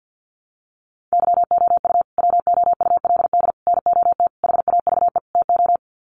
Morse code, heard as a keyed tone, about 50 wpm, 700 Hz. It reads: YOU WOULD NOT HAVE TO
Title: Sets of 5 Words Encore 50wpm